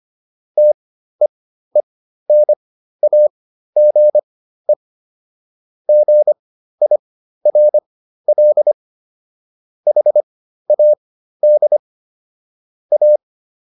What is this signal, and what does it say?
Morse code, heard as a keyed tone, about 25 wpm, spaced as for 13 wpm, 600 Hz. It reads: TEENAGE GIRL HAD A